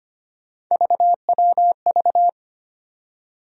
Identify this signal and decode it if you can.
Morse code, heard as a keyed tone, about 25 wpm, 700 Hz. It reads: VWV
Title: USA Call Sign Postfixes - 3 Letters 25wpm